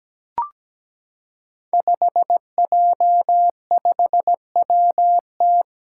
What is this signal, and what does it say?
Morse code, heard as a keyed tone, about 17 wpm, 700 Hz. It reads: 5J5WT
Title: Five Challenging Letter-Number Combinations 17wpm